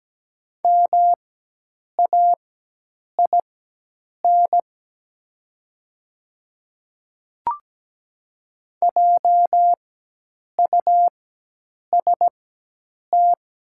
Morse code, heard as a keyed tone, about 17 wpm, spaced as for 8 wpm, 700 Hz. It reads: MAIN JUST
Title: 4x Character Spacing - Top 500 Words - 17wpm